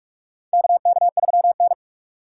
Morse code, heard as a keyed tone, about 46 wpm, 700 Hz. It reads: KK3N
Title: USA Call Signs - 4 Characters - 4x Character Spacing - 45wpm